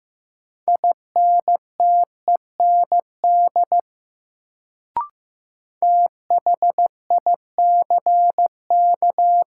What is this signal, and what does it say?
Morse code, heard as a keyed tone, about 15 wpm, 700 Hz. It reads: INTEND THICK
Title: Sentences from Top 1600 Words - New Word Drill - 15wpm